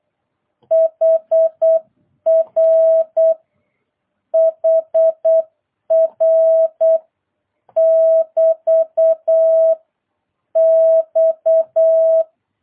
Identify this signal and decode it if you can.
Morse code, heard as a keyed tone, about 8 wpm, 650 Hz. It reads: HR HR <BT> X